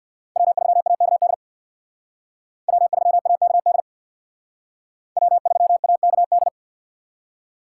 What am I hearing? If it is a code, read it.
Morse code, heard as a keyed tone, about 62 wpm, 700 Hz. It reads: W3AKD W3AKD W3AKD
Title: Speed Racing - USA 1x3 Call Signs 50wpm